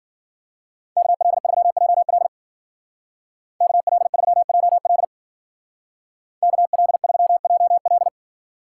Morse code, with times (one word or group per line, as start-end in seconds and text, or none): 1.0-2.3 KL3JL
3.6-5.0 KL3JL
6.4-8.1 KL3JL